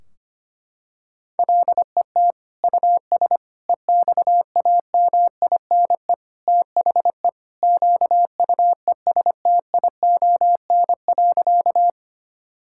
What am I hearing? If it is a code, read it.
Morse code, heard as a keyed tone, about 25 wpm, 700 Hz. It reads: LET US EXAMINE THE QUESTION.
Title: Sentences from Top 1600 Words 25wpm